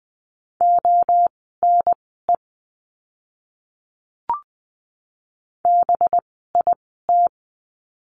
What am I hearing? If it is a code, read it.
Morse code, heard as a keyed tone, about 20 wpm, spaced as for 15 wpm, 700 Hz. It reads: ONE BIT
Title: 185 Lesson 30f - Letter b - Words 20wpm